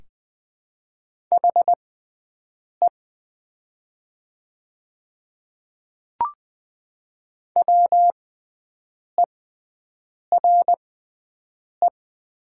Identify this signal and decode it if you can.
Morse code, heard as a keyed tone, about 20 wpm, spaced as for 7 wpm, 700 Hz. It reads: HE WERE
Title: 6x Character Spacing - Top 200 Words - 20wpm